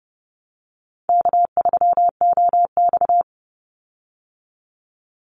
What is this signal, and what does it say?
Morse code, heard as a keyed tone, about 30 wpm, 700 Hz. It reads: K3OX
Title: Easy Call Signs Repeated 3x - 30wpm